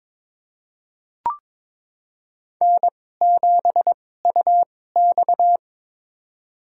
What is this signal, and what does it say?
Morse code, heard as a keyed tone, about 22 wpm, spaced as for 16 wpm, 700 Hz. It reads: N7UX
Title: USA Call Signs - 4 Characters - 2x Character Spacing - 22wpm